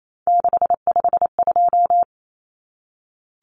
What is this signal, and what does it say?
Morse code, heard as a keyed tone, about 28 wpm, 700 Hz. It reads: 652
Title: Numbers - 3-Digit 28wpm